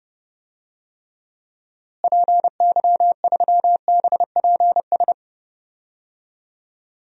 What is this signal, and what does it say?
Morse code, heard as a keyed tone, about 30 wpm, 700 Hz. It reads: PY3BPS
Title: DXCC-100 Call Signs - 6 Characters - Repeated 2x - 30wpm